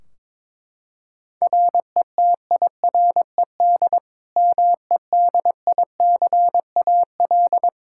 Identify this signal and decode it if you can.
Morse code, heard as a keyed tone, about 22 wpm, 700 Hz. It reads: RETIRED MEDICAL
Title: Practice QSO - Exchange 3 - More Info - 22wpm